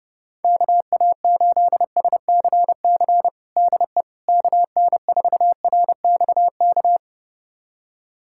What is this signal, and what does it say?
Morse code, heard as a keyed tone, about 30 wpm, 700 Hz. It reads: KA8SCC DE KN4RXK
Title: USA Callsign DE Callsign 30wpm